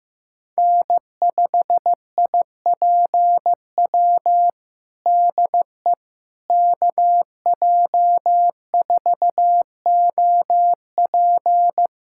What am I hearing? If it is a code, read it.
Morse code, heard as a keyed tone, about 15 wpm, 700 Hz. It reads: N5IPW DE KJ4OP
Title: USA Callsign DE Callsign - 5 Characters - 15wpm